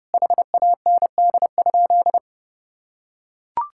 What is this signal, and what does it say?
Morse code, heard as a keyed tone, about 32 wpm, 700 Hz. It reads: HAND?
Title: Sentences from Top 1000 Words - Review Part 5 - 30wpm